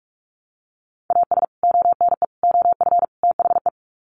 Morse code, heard as a keyed tone, about 44 wpm, 700 Hz. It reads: AS ONE OF THE
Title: Sets of 4 Words 45wpm